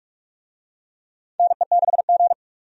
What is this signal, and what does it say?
Morse code, heard as a keyed tone, about 46 wpm, 700 Hz. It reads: NE6G
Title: USA Call Signs - 4 Characters - 45wpm